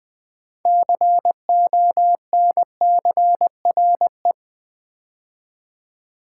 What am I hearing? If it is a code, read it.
Morse code, heard as a keyed tone, about 20 wpm, 700 Hz. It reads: CONCRE
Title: Build A Word - 3 Letters At A Time - 8-Letter Words - 20wpm